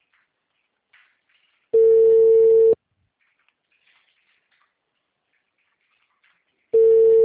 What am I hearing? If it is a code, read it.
Ringback tone (China, 450 Hz).